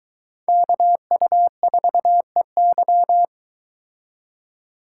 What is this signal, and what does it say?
Morse code, heard as a keyed tone, about 23 wpm, 700 Hz. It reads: KU4EY